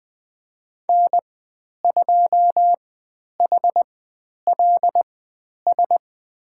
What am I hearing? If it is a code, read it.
Morse code, heard as a keyed tone, about 20 wpm, spaced as for 10 wpm, 700 Hz. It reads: N2HLS
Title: 095 Lesson 16g - Letter c - Call Signs 20wpm